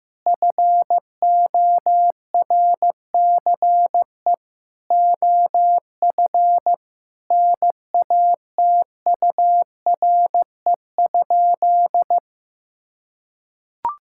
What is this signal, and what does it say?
Morse code, heard as a keyed tone, about 15 wpm, 700 Hz. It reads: FORCE OF NATURE?
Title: Sentences from Top 1000 Words - Review Part 5 - 15wpm